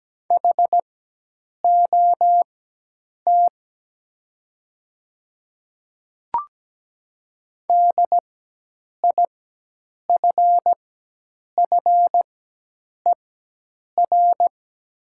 Morse code, heard as a keyed tone, about 17 wpm, spaced as for 8 wpm, 700 Hz. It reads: HOT DIFFER
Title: 4x Character Spacing - Top 200 Words - 17wpm